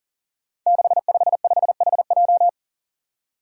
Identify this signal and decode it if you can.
Morse code, heard as a keyed tone, about 42 wpm, 700 Hz. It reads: 655HJ
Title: Five Challenging Letter-Number Combinations 40wpm